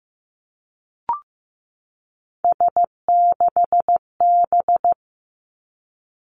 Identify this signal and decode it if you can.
Morse code, heard as a keyed tone, about 15 wpm, 700 Hz. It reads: S6B
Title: Three Challenging Letter-Number Combinations 15wpm